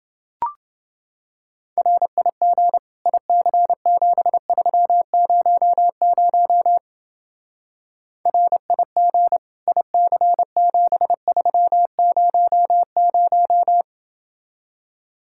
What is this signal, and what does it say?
Morse code, heard as a keyed tone, about 28 wpm, 700 Hz. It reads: RIG IC7300 RIG IC7300